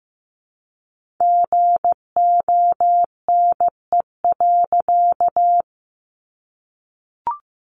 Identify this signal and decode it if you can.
Morse code, heard as a keyed tone, about 15 wpm, 700 Hz. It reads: GONE.
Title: A Sound Of Thunder 15wpm